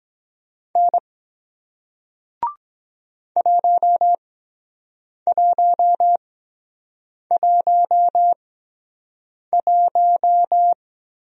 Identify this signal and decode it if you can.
Morse code, heard as a keyed tone, about 23 wpm, spaced as for 7 wpm, 700 Hz. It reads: N 1111